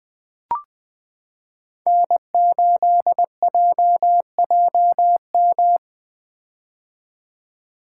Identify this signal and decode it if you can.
Morse code, heard as a keyed tone, about 20 wpm, 700 Hz. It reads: N8JJM
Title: 2023 POTA Hunter Call Signs - 20wpm